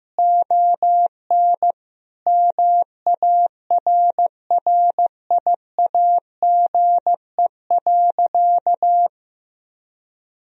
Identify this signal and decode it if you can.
Morse code, heard as a keyed tone, about 15 wpm, 700 Hz. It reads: ON MARRIAGE.
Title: Sentences from Top 1000 Words - Review Part 1 - 15wpm